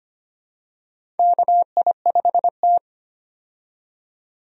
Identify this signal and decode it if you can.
Morse code, heard as a keyed tone, about 25 wpm, 700 Hz. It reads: KI5T
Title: USA Call Signs - 4 Characters - 25wpm